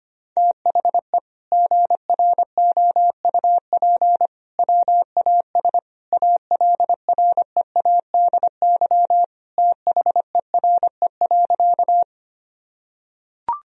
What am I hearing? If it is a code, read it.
Morse code, heard as a keyed tone, about 25 wpm, 700 Hz. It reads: THE GROUP WAS ALREADY THERE.
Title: Sentences from Top 1000 Words - Review Part 1 - 25wpm